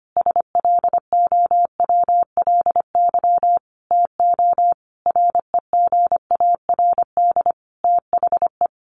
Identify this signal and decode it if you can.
Morse code, heard as a keyed tone, about 25 wpm, 700 Hz. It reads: SLOWLY TO REGARD THE
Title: A Sound Of Thunder 25wpm